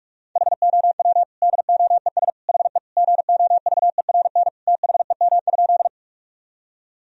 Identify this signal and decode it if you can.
Morse code, heard as a keyed tone, about 46 wpm, 700 Hz. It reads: HOW DOES HE GOVERN THEM?
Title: Sentences from Top 1000 Words - Review Part 2 - 45wpm